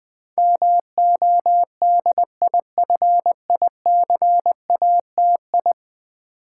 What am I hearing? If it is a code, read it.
Morse code, heard as a keyed tone, about 20 wpm, 700 Hz. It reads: MODIFICATI
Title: Build A Word - 5 Letters At A Time - 12-Letter Words - 20wpm